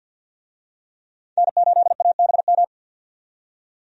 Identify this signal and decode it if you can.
Morse code, heard as a keyed tone, about 50 wpm, 700 Hz. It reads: N8ABM
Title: USA Call Signs - 5 Characters - 50wpm